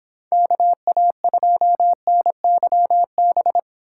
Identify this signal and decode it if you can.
Morse code, heard as a keyed tone, about 26 wpm, 700 Hz. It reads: KA2NYB